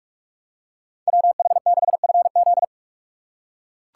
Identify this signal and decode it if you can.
Morse code, heard as a keyed tone, about 42 wpm, 700 Hz. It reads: WH6FZ